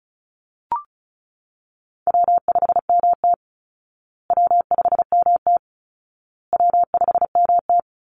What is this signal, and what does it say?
Morse code, heard as a keyed tone, about 34 wpm, 700 Hz. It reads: W5MT W5MT W5MT
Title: Easy Call Signs Repeated 3x - 35wpm